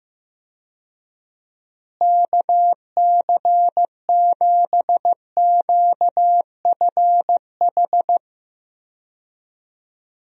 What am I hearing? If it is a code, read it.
Morse code, heard as a keyed tone, about 15 wpm, 700 Hz. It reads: KC7QFH
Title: USA Call Signs - 6 Characters - Repeated 3x - 15wpm